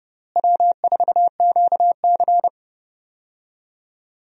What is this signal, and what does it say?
Morse code, heard as a keyed tone, about 30 wpm, 700 Hz. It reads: W4QC
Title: USA Call Signs - 4 Characters - Repeated 2x - 30wpm